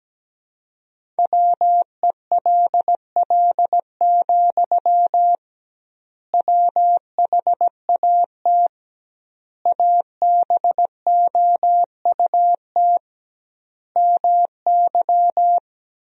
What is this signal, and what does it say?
Morse code, heard as a keyed tone, about 17 wpm, 700 Hz. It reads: WELL, WHAT ABOUT MY
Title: 2x Word-Spacing - Sentences from Top 500 Words - 17wpm